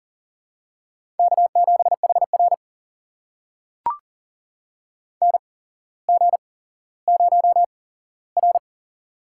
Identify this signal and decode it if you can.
Morse code, heard as a keyed tone, about 38 wpm, spaced as for 17 wpm, 700 Hz. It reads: K7HR NG0R
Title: USA Call Signs - 8x Character Spacing - 40wpm